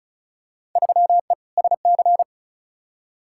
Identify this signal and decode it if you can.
Morse code, heard as a keyed tone, about 36 wpm, 700 Hz. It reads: 3E SC
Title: Field Day Exchanges Encore - 35wpm